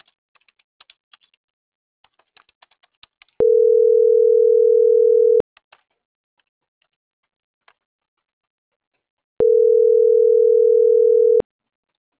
Ringback tone (North America), 440 + 480 Hz.